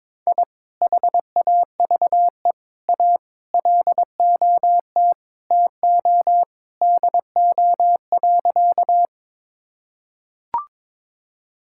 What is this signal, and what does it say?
Morse code, heard as a keyed tone, about 22 wpm, 700 Hz. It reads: I HAVE A LOT TO DO.
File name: practice / 22wpm / Sentences with 2-Word Phrases - 22wpm